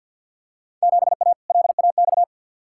Morse code, heard as a keyed tone, about 53 wpm, 700 Hz. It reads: 7A LAX
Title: Field Day Exchanges Encore - 50wpm